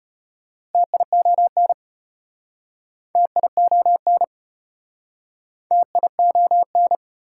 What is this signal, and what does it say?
Morse code, heard as a keyed tone, about 36 wpm, 700 Hz. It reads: TION TION TION